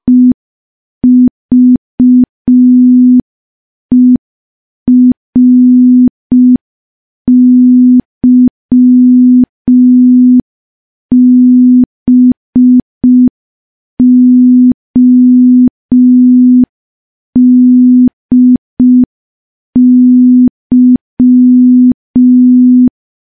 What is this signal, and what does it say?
Morse code, heard as a keyed tone, about 5 wpm, 250 Hz. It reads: EVERYBODY
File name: morse / morse-training-05WPM